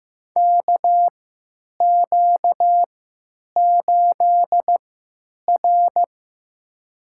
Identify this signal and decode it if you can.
Morse code, heard as a keyed tone, about 15 wpm, spaced as for 9 wpm, 700 Hz. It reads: KQ8R
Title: USA Call Signs - 4 Characters - 3x Character Spacing - 15wpm